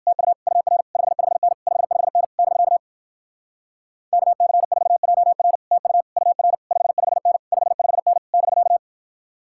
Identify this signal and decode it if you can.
Morse code, heard as a keyed tone, about 66 wpm, 700 Hz. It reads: TU UR 55N 55N <BK> KC4JR TU UR 55N 55N <BK>